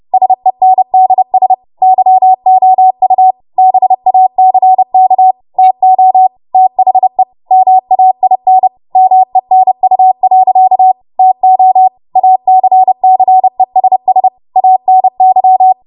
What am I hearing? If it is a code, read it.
Morse code, heard as a keyed tone, about 30 wpm, 750 Hz. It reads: SENDS YOU BACK TO THE MAIN MENU. TO ACCESS ANY
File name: Morse / 30WPM